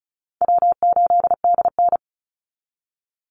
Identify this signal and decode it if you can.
Morse code, heard as a keyed tone, about 34 wpm, 700 Hz. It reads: W8DN
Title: Call Signs - 35wpm